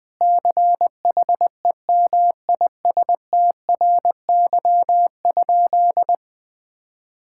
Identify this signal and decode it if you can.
Morse code, heard as a keyed tone, about 21 wpm, 700 Hz. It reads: CHEMISTRY?
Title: Sentences from Top 1000 Words - Review Part 6 - 20wpm